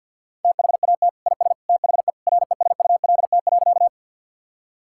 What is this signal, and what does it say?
Morse code, heard as a keyed tone, about 50 wpm, 700 Hz. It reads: THAT IS THE RESULT.
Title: Sentences from Top 1000 Words - Review Part 5 - 50wpm